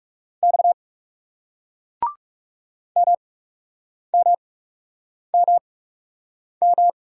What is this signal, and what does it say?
Morse code, heard as a keyed tone, about 40 wpm, spaced as for 8 wpm, 700 Hz. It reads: X MMMM